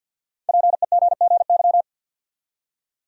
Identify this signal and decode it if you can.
Morse code, heard as a keyed tone, about 48 wpm, 700 Hz. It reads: PEGGY
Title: USA Common Names - 50wpm